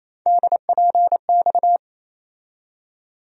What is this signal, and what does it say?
Morse code, heard as a keyed tone, about 28 wpm, 700 Hz. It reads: DPX
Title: USA Call Sign Postfixes - 3 Letters 28wpm